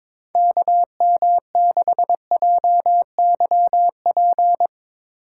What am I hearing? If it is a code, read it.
Morse code, heard as a keyed tone, about 21 wpm, 700 Hz. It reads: KM6JYP
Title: USA Call Signs - 4x Character Spacing - 22wpm